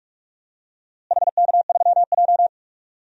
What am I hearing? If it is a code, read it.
Morse code, heard as a keyed tone, about 46 wpm, 700 Hz. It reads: HK3J
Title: DXCC-100 Call Signs - 4 Characters - Repeated 2x - 45wpm